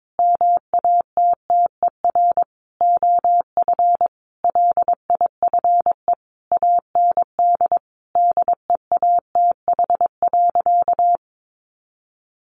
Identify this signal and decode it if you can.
Morse code, heard as a keyed tone, about 22 wpm, 700 Hz. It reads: MATTER OF LIFE AND DEATH.
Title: Sentences from Top 1100 Words 22wpm